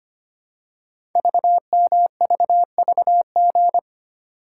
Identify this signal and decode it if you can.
Morse code, heard as a keyed tone, about 25 wpm, 700 Hz. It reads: VMVVG
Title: Code Groups - 5 Letters - 25wpm